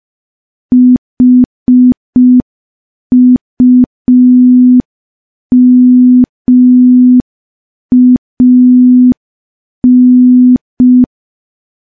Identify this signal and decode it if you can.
Morse code, heard as a keyed tone, about 5 wpm, 250 Hz. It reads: HUMAN
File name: morse / morse-training-05WPM